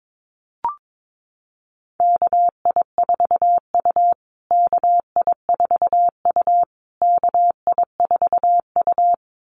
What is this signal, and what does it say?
Morse code, heard as a keyed tone, about 22 wpm, 700 Hz. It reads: KI4U KI4U KI4U
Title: USA Call Signs Repeated 3x 22wpm